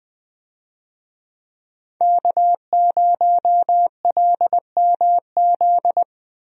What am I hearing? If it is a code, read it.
Morse code, heard as a keyed tone, about 20 wpm, 700 Hz. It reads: K0LMZ
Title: USA Call Signs - 5 Characters - Repeated 3x - 20wpm